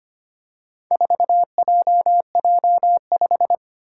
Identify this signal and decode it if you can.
Morse code, heard as a keyed tone, about 25 wpm, 700 Hz. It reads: 4JJ5
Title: Four Challenging Letter-Number Combinations 25wpm